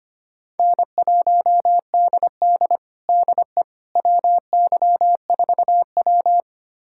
Morse code, heard as a keyed tone, about 25 wpm, 700 Hz. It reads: N1DD DE WY4W
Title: USA Callsign DE Callsign - 4 Characters - 25wpm